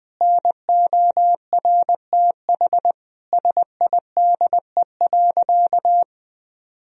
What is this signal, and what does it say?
Morse code, heard as a keyed tone, about 21 wpm, 700 Hz. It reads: NORTH SIDE.
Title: Sentences from Top 1000 Words - Review Part 1 - 20wpm